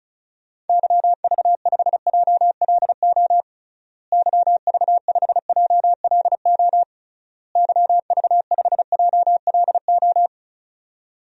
Morse code, heard as a keyed tone, about 36 wpm, 700 Hz. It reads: YV5JLO YV5JLO YV5JLO
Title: DXCC-100 Call Signs - 6 Characters - Repeated 3x - 35wpm